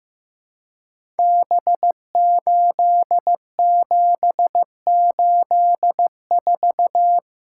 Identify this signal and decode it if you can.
Morse code, heard as a keyed tone, about 15 wpm, 700 Hz. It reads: B8784
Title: Code Groups - 5 Letter:Numbers - 15wpm